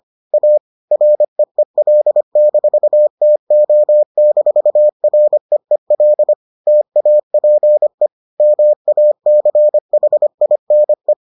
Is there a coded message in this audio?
Yes: Morse code from 0.3 to 11.1 s, heard as a keyed tone, about 25 wpm, 600 Hz. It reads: A REEL-TO-REEL TAPE MACHINE